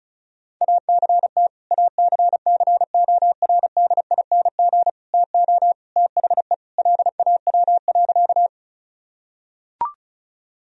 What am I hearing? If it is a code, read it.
Morse code, heard as a keyed tone, about 34 wpm, 700 Hz. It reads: ACT ACCORDING TO THE LAW.
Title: Sentences from Top 1000 Words - Review Part 4 - 35wpm